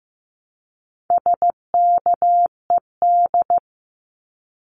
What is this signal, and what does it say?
Morse code, heard as a keyed tone, about 15 wpm, 700 Hz. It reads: SKED
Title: Abbreviations Repeated 3x - 15wpm